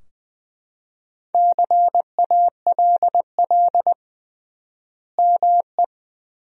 Morse code, heard as a keyed tone, about 21 wpm, 700 Hz. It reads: CALL ME
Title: 3x Word-Spacing - Sentences from Top 200 Words - 20wpm